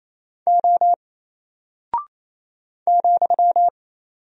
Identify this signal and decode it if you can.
Morse code, heard as a keyed tone, about 27 wpm, 700 Hz. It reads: O ,
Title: Single Character (Letter:Number:Punctuation:Prosign) 28wpm